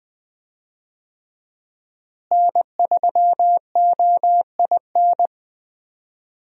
Morse code, heard as a keyed tone, about 20 wpm, 700 Hz. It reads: N3OIN
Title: USA Call Signs - 5 Characters - Repeated 3x - 20wpm